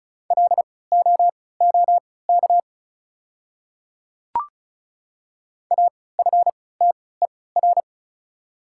Morse code, heard as a keyed tone, about 36 wpm, spaced as for 20 wpm, 690 Hz. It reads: LOOK AFTER
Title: 3x Character Spacing - Top 500 Words - 35wpm